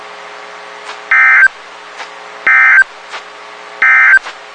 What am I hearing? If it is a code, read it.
Emergency Alert System (SAME) tones, heard as harsh buzzing bursts. It is heard from 1.0 to 4.3 s.
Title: WXK91 NOAA Weather Radio